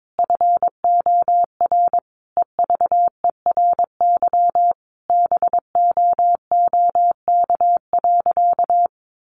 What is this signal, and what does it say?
Morse code, heard as a keyed tone, about 22 wpm, 700 Hz. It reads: FOR EVERY BOOK.
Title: Sentences from Top 1500 Words 22wpm